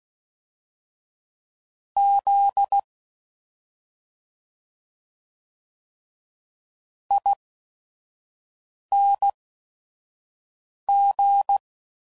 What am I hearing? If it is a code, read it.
Morse code, heard as a keyed tone, about 16 wpm, spaced as for 5 wpm, 800 Hz. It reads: Z ING